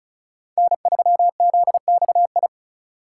Morse code, heard as a keyed tone, about 35 wpm, 700 Hz. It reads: N3ZXI
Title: USA Call Signs - 5 Characters - 3x Character Spacing - 35wpm